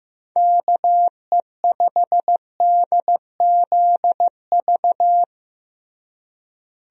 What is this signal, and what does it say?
Morse code, heard as a keyed tone, about 15 wpm, 700 Hz. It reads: KE5DZV
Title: USA Call Signs - 6 Characters - Repeated 2x - 15wpm